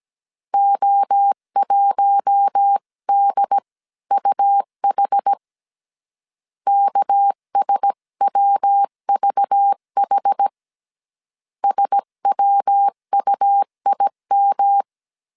Morse code, heard as a keyed tone, about 17 wpm, 800 Hz. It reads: O1DUH KSWVH SWUIM
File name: CW / DET / 1 6 1500C S C 16wpm 2733